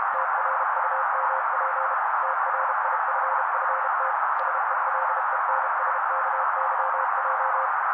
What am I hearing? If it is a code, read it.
Morse code, heard as a keyed tone, about 32 wpm, 530 Hz. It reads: TRUMP TRIFFT SELENSKYJ
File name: daily / 31wpm noise